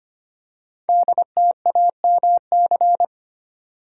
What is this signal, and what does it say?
Morse code, heard as a keyed tone, about 25 wpm, 700 Hz. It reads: DTAMC
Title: Code Groups - 5 Letters - 25wpm